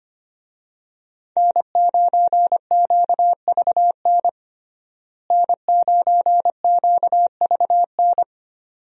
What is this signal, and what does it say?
Morse code, heard as a keyed tone, about 25 wpm, 700 Hz. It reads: N9QVN N9QVN
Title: USA Call Signs - 5 Characters - Repeated 2x - 25wpm